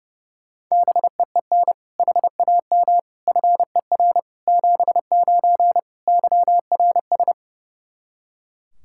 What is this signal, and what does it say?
Morse code, heard as a keyed tone, about 29 wpm, 700 Hz. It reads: BEEN HAM FER 79 YRS
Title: Practice QSO - Exchange 3 - More Info - 30wpm